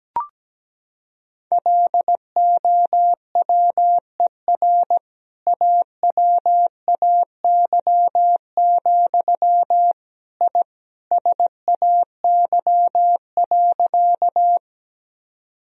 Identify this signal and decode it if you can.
Morse code, heard as a keyed tone, about 17 wpm, 700 Hz. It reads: LOWER AWAY, I SAY.
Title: Sentences from Top 1000 Words - Review Part 4 - 17wpm